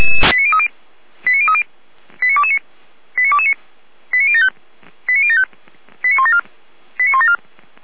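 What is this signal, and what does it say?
Five-tone selective call (ZVEI), heard as quick runs of five tones, tones of about 70 ms. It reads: 89030 89030 89229 89229 89075 89075 89174 89174
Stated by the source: ZVEI five-tone call received on SDR.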